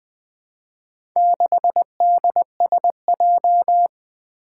Morse code, heard as a keyed tone, about 20 wpm, 700 Hz. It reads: 6DSJ
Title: Four Challenging Letter-Number Combinations 20wpm